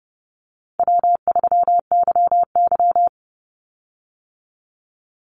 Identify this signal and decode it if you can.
Morse code, heard as a keyed tone, about 30 wpm, 700 Hz. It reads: W3YY
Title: Easy Call Signs Repeated 3x - 30wpm